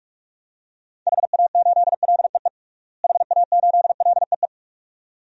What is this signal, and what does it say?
Morse code, heard as a keyed tone, about 41 wpm, 700 Hz. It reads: HA8LEE HA8LEE
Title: DXCC-100 Call Signs - 6 Characters - Repeated 2x - 45wpm